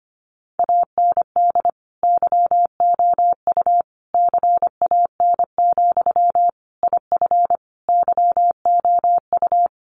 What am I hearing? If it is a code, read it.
Morse code, heard as a keyed tone, about 25 wpm, 700 Hz. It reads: AND YOU CAN, IF YOU
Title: Sentences from Top 1500 Words 25wpm